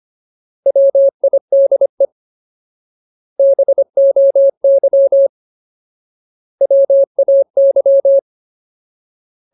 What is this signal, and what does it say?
Morse code, heard as a keyed tone, about 25 wpm, 550 Hz. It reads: WIDE BOY WAY
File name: Ultimate Word List / 25WPM Ultimate